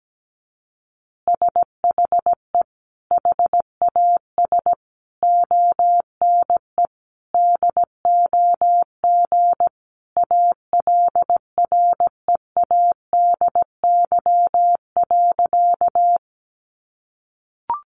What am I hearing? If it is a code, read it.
Morse code, heard as a keyed tone, about 17 wpm, 700 Hz. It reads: SHE HAS ONE DOG ALREADY.
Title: Easy Phrases 17wpm